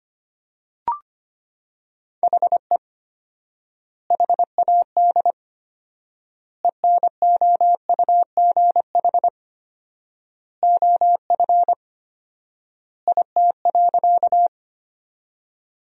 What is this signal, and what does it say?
Morse code, heard as a keyed tone, about 25 wpm, 700 Hz. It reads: HE HAD ENOUGH OF IT.
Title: 4x Word-Spacing - Sentences from Top 400 Words - 25wpm